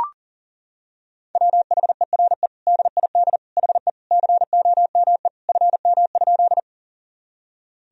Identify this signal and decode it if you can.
Morse code, heard as a keyed tone, about 37 wpm, 700 Hz. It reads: WHERE DID HE COME FM?
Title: Sentences from Top 200 Words and Abbreviations 40wpm